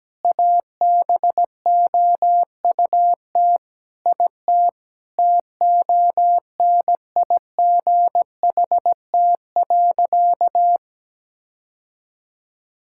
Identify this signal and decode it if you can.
Morse code, heard as a keyed tone, about 17 wpm, 700 Hz. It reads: ABOUT IT TONIGHT.
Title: Sentences from Top 1000 Words - Review Part 5 - 17wpm